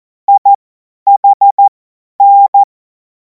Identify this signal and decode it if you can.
Morse code, heard as a keyed tone, about 14 wpm, spaced as for 10 wpm, 800 Hz. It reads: IHN
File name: Morsekurs / Witze / Witz 23. Juni 2026 50 F70